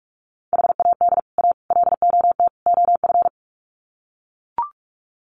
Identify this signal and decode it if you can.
Morse code, heard as a keyed tone, about 44 wpm, 700 Hz. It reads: HAD A LOT OF
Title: Sets of 4 Words 45wpm